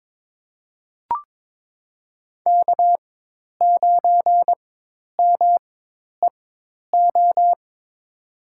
Morse code, heard as a keyed tone, about 22 wpm, spaced as for 10 wpm, 700 Hz. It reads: K9MEO
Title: USA Call Signs - 5 Characters - 4x Character Spacing - 22wpm